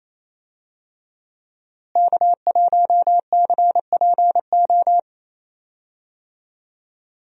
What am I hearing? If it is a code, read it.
Morse code, heard as a keyed tone, about 28 wpm, 700 Hz. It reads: K1CPO
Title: USA Call Signs - 5 Characters - Repeated 2x - 28wpm